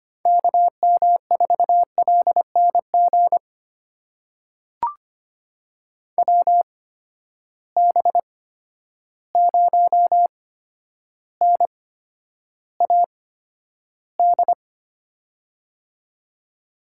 Morse code, heard as a keyed tone, about 25 wpm, spaced as for 11 wpm, 700 Hz. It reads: KM4LNG WB0NAD
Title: USA Call Signs - 6 Characters - 8x Character Spacing - 25wpm